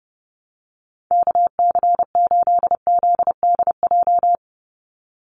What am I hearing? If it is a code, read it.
Morse code, heard as a keyed tone, about 29 wpm, 700 Hz. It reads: KC8ZDJ
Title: USA Call Signs 30wpm